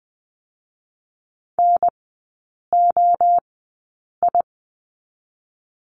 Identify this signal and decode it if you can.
Morse code, heard as a keyed tone, about 20 wpm, spaced as for 8 wpm, 700 Hz. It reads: NOI
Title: 041 Lesson 8e - Number 1 - Three Characters 20wpm